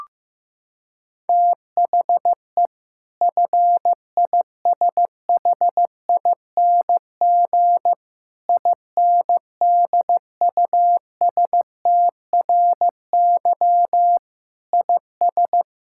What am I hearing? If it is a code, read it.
Morse code, heard as a keyed tone, about 16 wpm, 700 Hz. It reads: THE FISHING INDUSTRY IS
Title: Sentences from Top 1000 Words - Review Part 6 - 15wpm